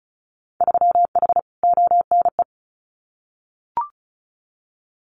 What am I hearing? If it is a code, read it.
Morse code, heard as a keyed tone, about 36 wpm, 700 Hz. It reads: 3H ONE
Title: Winter Field Day Exchanges Encore 35wpm